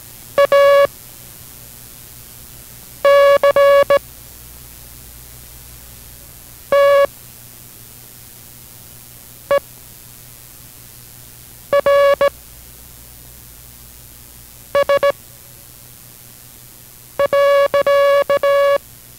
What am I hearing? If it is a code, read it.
Morse code, heard as a keyed tone, about 17 wpm, 560 Hz. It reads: A C T E R S .